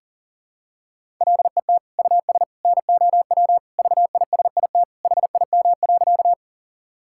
Morse code, heard as a keyed tone, about 40 wpm, 700 Hz. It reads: LET US NOW VISIT HIM.
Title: Sentences from Top 1000 Words - Review Part 6 - 40wpm